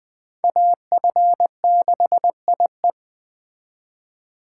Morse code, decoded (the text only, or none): AF6IE